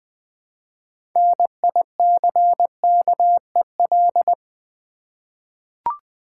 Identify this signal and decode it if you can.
Morse code, heard as a keyed tone, about 20 wpm, 700 Hz. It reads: NICKEL
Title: 6 Letter Words - First 3 Letters Given - 20wpm